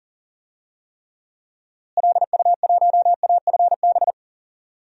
Morse code, heard as a keyed tone, about 40 wpm, 700 Hz. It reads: LU1AFB
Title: DXCC-100 Call Signs - 6 Characters - Repeated 3x - 40wpm